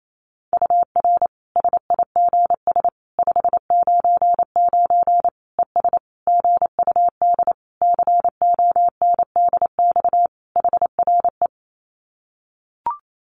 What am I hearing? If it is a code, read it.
Morse code, heard as a keyed tone, about 28 wpm, 700 Hz. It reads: UR SIGS 599 ES GUD CONDX HRE
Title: Common QSO Phrases 28wpm